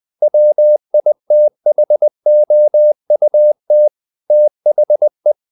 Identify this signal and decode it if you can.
Morse code, heard as a keyed tone, about 20 wpm, 600 Hz. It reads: WITHOUT THE
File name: the old man and the sea / the old man and the sea part 1 20wpm